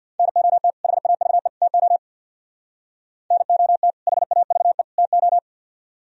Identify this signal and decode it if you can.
Morse code, heard as a keyed tone, about 60 wpm, 700 Hz. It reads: NOT HAVE TO NOT HAVE TO
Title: Speed-Racing - Sets of 3 Words 50wpm